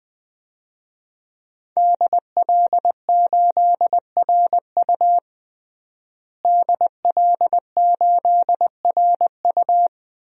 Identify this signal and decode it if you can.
Morse code, heard as a keyed tone, about 20 wpm, 700 Hz. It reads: DL8RU DL8RU
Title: DXCC-100 Call Signs - 5 Characters - Repeated 2x - 20wpm